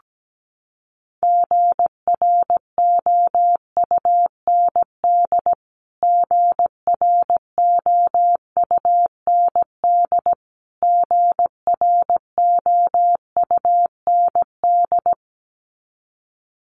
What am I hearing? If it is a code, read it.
Morse code, heard as a keyed tone, about 17 wpm, 700 Hz. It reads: GROUND GROUND GROUND
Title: Top 500 Words Repeated 17wpm